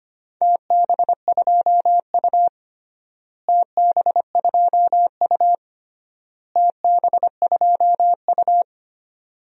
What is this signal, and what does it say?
Morse code, heard as a keyed tone, about 25 wpm, 700 Hz. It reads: TB2U TB2U TB2U
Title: DXCC-100 Call Signs - 4 Characters - Repeated 3x - 25wpm